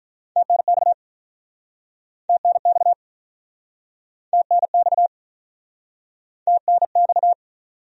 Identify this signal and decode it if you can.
Morse code, heard as a keyed tone, about 45 wpm, 700 Hz. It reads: TNX TNX TNX TNX